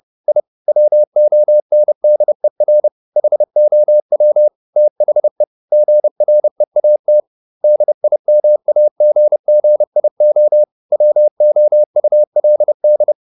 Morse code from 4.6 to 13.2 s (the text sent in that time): THE GREAT DIMAGGIO WOULD